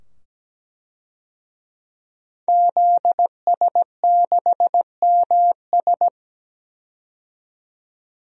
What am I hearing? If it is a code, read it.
Morse code, heard as a keyed tone, about 17 wpm, 700 Hz. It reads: ZS6MS
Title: DXCC-100 Call Signs - 5 Characters - 17wpm